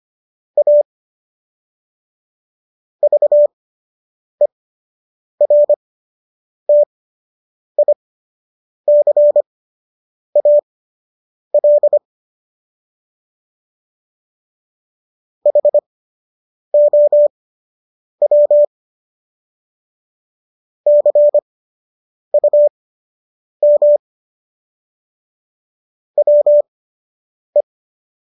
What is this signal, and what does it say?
Morse code, heard as a keyed tone, about 25 wpm, spaced as for 8 wpm, 590 Hz. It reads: A VERTICAL HOW CUM WE
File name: qso006 08